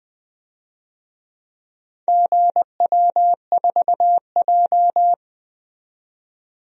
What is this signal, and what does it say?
Morse code, heard as a keyed tone, about 20 wpm, 700 Hz. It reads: GW4J
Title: DXCC-100 Call Signs - 4 Characters - 20wpm